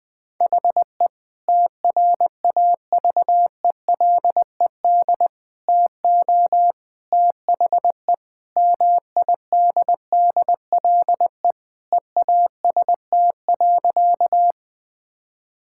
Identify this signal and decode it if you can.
Morse code, heard as a keyed tone, about 20 wpm, 700 Hz. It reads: HE TRAVELED TO THE MIDDLE EAST.
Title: Sentences with 3-Word Phrases - Spoken Beginning - 20wpm